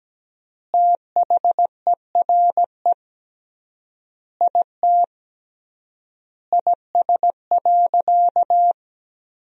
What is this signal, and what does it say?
Morse code, heard as a keyed tone, about 17 wpm, 700 Hz. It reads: THERE IT IS.
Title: 3x Word-Spacing - Sentences from Top 500 Words - 17wpm